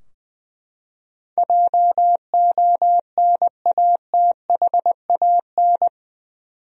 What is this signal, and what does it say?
Morse code, heard as a keyed tone, about 20 wpm, 700 Hz. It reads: JONATHAN
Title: USA Common Names - 20wpm